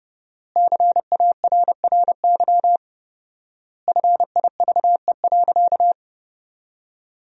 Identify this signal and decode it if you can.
Morse code, heard as a keyed tone, about 30 wpm, 700 Hz. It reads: CARRY FIVE.
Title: 4x Word-Spacing - Sentences from Top 400 Words - 30wpm